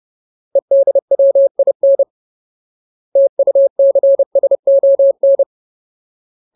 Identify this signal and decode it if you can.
Morse code, heard as a keyed tone, about 30 wpm, 550 Hz. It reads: EDWIN TUCSON
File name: Ultimate Word List / 30WPM Ultimate